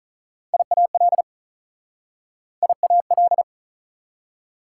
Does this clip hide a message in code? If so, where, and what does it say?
Yes: Morse code from 0.5 to 3.4 s, heard as a keyed tone, about 47 wpm, 700 Hz. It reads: IAL IAL